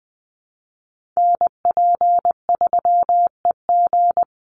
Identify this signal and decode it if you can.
Morse code, heard as a keyed tone, about 20 wpm, 700 Hz. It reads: NP3EG
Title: USA Call Signs 20wpm